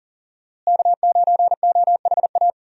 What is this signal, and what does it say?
Morse code, heard as a keyed tone, about 40 wpm, 700 Hz. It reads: K9OHA
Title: USA Call Signs - 5 Characters - Repeated 3x - 40wpm